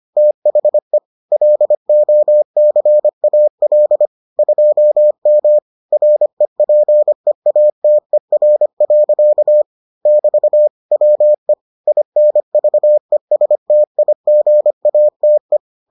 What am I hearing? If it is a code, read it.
Morse code, heard as a keyed tone, about 25 wpm, 600 Hz. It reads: THE LOCAL 2M REPEATER. <BT> WE INVESTIGATE